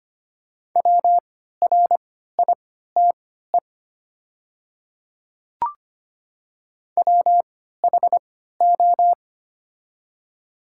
Morse code, heard as a keyed tone, about 25 wpm, spaced as for 14 wpm, 700 Hz. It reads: WRITE WHO
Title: 3x Character Spacing - Top 100 Words - 25wpm